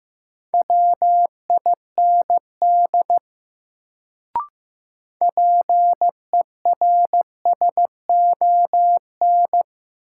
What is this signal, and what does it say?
Morse code, heard as a keyed tone, about 15 wpm, 700 Hz. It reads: WIND PERSON
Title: Sentences from Top 400 Words - New Word Drill - 15wpm